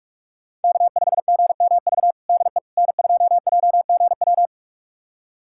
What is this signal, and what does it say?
Morse code, heard as a keyed tone, about 46 wpm, 700 Hz. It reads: K5GMV DE N2JGW
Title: USA Callsign DE Callsign - 5 Characters - 45wpm